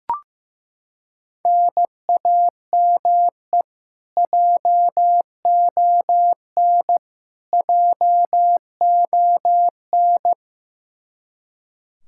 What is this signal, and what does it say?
Morse code, heard as a keyed tone, about 15 wpm, 700 Hz. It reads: NAME JON JON
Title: K1USN SST QSO Practice 15wpm